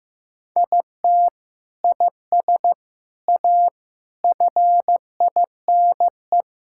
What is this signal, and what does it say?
Morse code, heard as a keyed tone, about 15 wpm, 700 Hz. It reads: IT IS A FINE
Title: Sentences from Top 1600 Words 15wpm